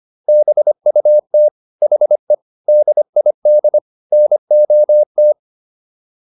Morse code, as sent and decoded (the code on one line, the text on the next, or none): -... ..- - / .... / -.. .. -.. / -. --- -
BUT HE DID NOT